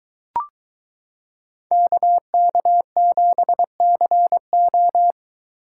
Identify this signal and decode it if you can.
Morse code, heard as a keyed tone, about 22 wpm, 700 Hz. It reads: KK7CO